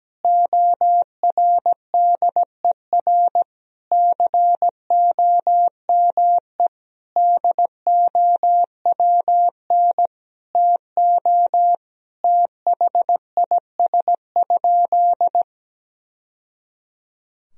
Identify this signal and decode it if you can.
Morse code, heard as a keyed tone, about 17 wpm, 700 Hz. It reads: ORDER COME DOWN TO THIS?
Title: Sentences from Top 1000 Words - Review Part 3 - 17wpm